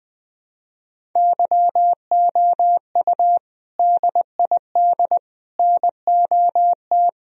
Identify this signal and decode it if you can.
Morse code, heard as a keyed tone, about 20 wpm, 700 Hz. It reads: YOU DID NOT
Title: Sentences from Top 1000 Words - Review Part 1 - 20wpm